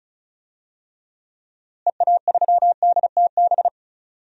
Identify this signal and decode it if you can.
Morse code, heard as a keyed tone, about 36 wpm, 700 Hz. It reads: EA3DTB
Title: DXCC-100 Call Signs - 6 Characters - Repeated 2x - 35wpm